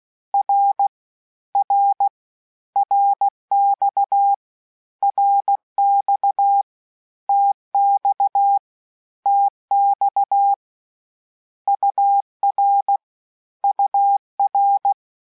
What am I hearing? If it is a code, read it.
Morse code, heard as a keyed tone, about 16 wpm, 800 Hz. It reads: R R RX RX TX TX UR UR
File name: cours-du-lundi files / cours-1580-abreviations-mpm-15